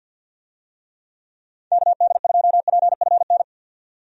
Morse code, heard as a keyed tone, about 52 wpm, 700 Hz. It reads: KD2PFN
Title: USA Call Signs - 6 Characters - 50wpm